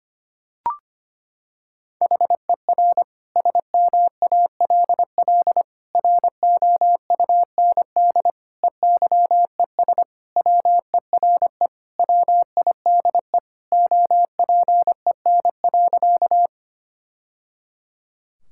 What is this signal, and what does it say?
Morse code, heard as a keyed tone, about 25 wpm, 700 Hz. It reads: HER SMALL ROUND EYES WERE WIDE OPEN.
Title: Sentences from Top 1000 Words - Review Part 6 - 25wpm